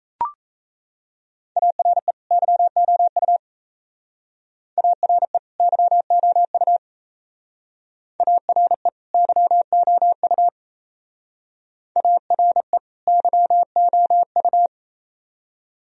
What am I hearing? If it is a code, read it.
Morse code, heard as a keyed tone, about 36 wpm, 690 Hz. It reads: ARE YOU ARE YOU ARE YOU ARE YOU